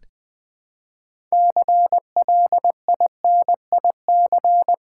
Morse code, heard as a keyed tone, about 20 wpm, 700 Hz. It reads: CLINIC
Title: Build A Word - 2 Letters At A Time - 6-Letter Words - 20wpm